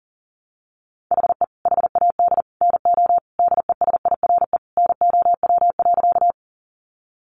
Morse code, heard as a keyed tone, about 40 wpm, 700 Hz. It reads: HE HAD NO DESIRE NOW.
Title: Sentences from Top 1100 Words 40wpm